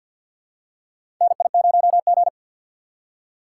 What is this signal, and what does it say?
Morse code, heard as a keyed tone, about 50 wpm, 700 Hz. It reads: NI0G
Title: USA Call Signs - 4 Characters - 6x Character Spacing - 50wpm